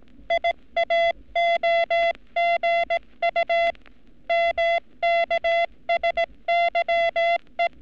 Morse code, heard as a keyed tone, about 18 wpm, 680 Hz. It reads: IAOGU MKSYE
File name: training / NAVPERS 11224-11235 / NAVPERS 11233 RA RB